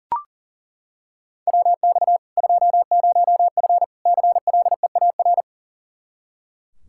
Morse code, heard as a keyed tone, about 40 wpm, 700 Hz. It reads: WX 20F CLEAR
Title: K1USN SST QSO Practice 40wpm